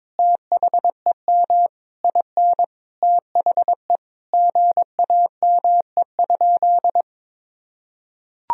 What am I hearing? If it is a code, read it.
Morse code, heard as a keyed tone, about 22 wpm, 700 Hz. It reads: THEM IN THE GAME?
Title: Sentences from Top 1000 Words - Review Part 4 - 22wpm